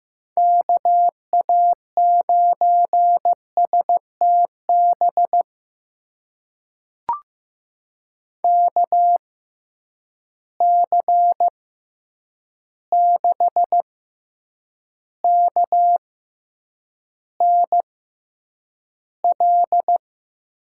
Morse code, heard as a keyed tone, about 15 wpm, spaced as for 8 wpm, 700 Hz. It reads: KA9STB KC6KNL
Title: USA Call Signs - 6 Characters - 6x Character Spacing - 15wpm